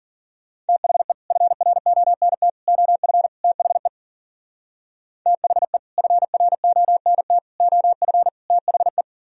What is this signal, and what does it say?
Morse code, heard as a keyed tone, about 52 wpm, 700 Hz. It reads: THE FRONT OF THE THE FRONT OF THE